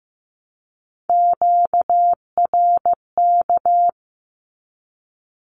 Morse code, heard as a keyed tone, about 15 wpm, 700 Hz. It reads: QRK
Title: Q-Codes 15wpm